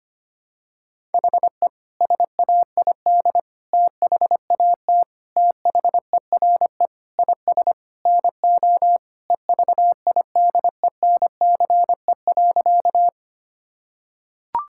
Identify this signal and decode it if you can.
Morse code, heard as a keyed tone, about 25 wpm, 700 Hz. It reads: HE SAID THAT THERE IS NO EVIDENCE.
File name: practice / 25wpm / Sentences with 4-Word Phrases - 25wpm